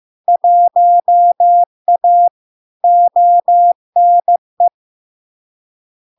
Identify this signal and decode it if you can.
Morse code, heard as a keyed tone, about 15 wpm, 700 Hz. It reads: 1A ONE
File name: LARK-FD-2022-Log-in-CW-15wpm-700-Hz